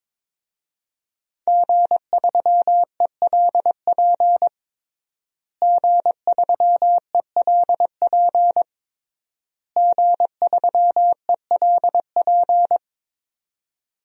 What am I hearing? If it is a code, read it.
Morse code, heard as a keyed tone, about 22 wpm, 700 Hz. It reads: G3ELP G3ELP G3ELP
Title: DXCC-100 Call Signs - 5 Characters - Repeated 3x - 22wpm